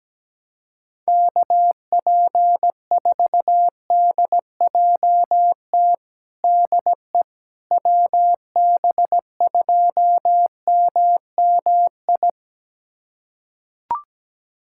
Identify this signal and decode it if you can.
Morse code, heard as a keyed tone, about 17 wpm, 700 Hz. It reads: KP4DJT DE WB2MMI
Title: USA Callsign DE Callsign - 6 Characters - 17wpm